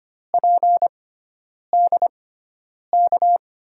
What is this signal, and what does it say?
Morse code, heard as a keyed tone, about 25 wpm, spaced as for 9 wpm, 700 Hz. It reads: PDK
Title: USA Call Signs - 5 Characters - 6x Character Spacing - 25wpm